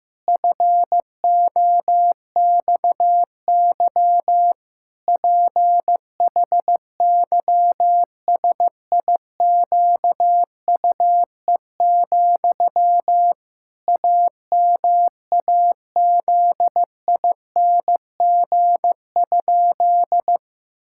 Morse code, heard as a keyed tone, about 15 wpm, 700 Hz. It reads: FOXY PHYSIQUE, AMAZING?
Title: Pangram Challenge v2 - 15wpm